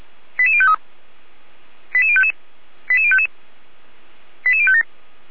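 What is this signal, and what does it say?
Five-tone selective call (ZVEI), heard as quick runs of five tones, tones of about 70 ms. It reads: 88053 88055 88055 88057
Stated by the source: Five-tone ZVEI selective call received by SDR, from audio.f85360.feuerwehr-orlamuende.org (Feuerwehr Orlamünde).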